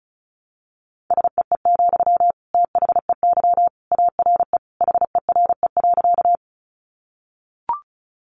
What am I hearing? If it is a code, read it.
Morse code, heard as a keyed tone, about 34 wpm, 700 Hz. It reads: SEE, THEY ARE HERE.
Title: Sentences from Top 200 Words 35wpm